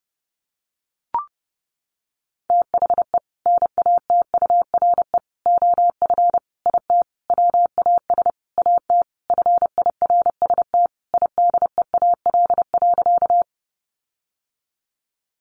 Morse code, heard as a keyed tone, about 30 wpm, 700 Hz. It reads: THE NATURE OF IT WAS AT FIRST IDEAL.
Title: Sentences from Top 1400 Words 30wpm